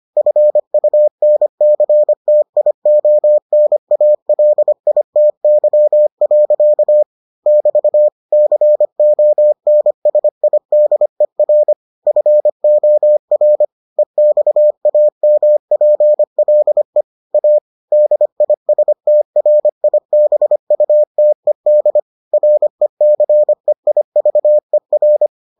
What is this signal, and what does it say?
Morse code, heard as a keyed tone, about 25 wpm, 600 Hz. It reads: FUNCTIONALITY. <BT> CONSIDER FOR EXAMPLE A DISTRIBUTED RECEIVER